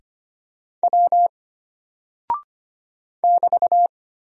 Morse code, heard as a keyed tone, about 25 wpm, 700 Hz. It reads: W <BT>
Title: Single Character (Letter:Number:Punctuation:Prosign) Alt 25wpm